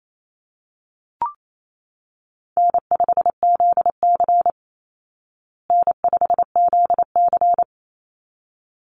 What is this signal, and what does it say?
Morse code, heard as a keyed tone, about 28 wpm, 700 Hz. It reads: N5ZC N5ZC
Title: Easy Call Signs Repeated 2x - 28wpm